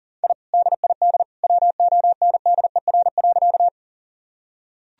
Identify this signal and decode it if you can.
Morse code, heard as a keyed tone, about 40 wpm, 700 Hz. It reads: I DID WONDER.
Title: Sentences from Top 1000 Words - Review Part 3 - 40wpm